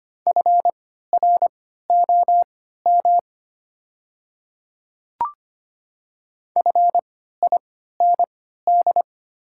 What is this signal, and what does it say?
Morse code, heard as a keyed tone, about 25 wpm, spaced as for 14 wpm, 690 Hz. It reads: FROM FIND
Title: 3x Character Spacing - Top 100 Words - 25wpm